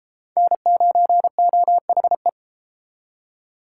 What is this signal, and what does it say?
Morse code, heard as a keyed tone, about 30 wpm, 700 Hz. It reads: N9OHE